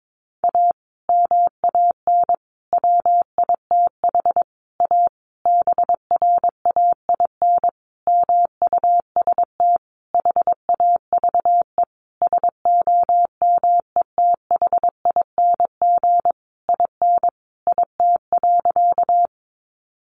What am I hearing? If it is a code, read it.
Morse code, heard as a keyed tone, about 22 wpm, 700 Hz. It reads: A MAN WITH A BRAIN MUST HAVE SOMETHING IN IT.
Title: Sentences from Top 1100 Words 22wpm